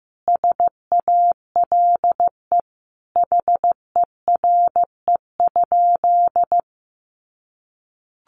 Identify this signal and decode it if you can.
Morse code, heard as a keyed tone, about 15 wpm, 700 Hz. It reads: SALE HERE?
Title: Sentences from Top 1400 Words 15wpm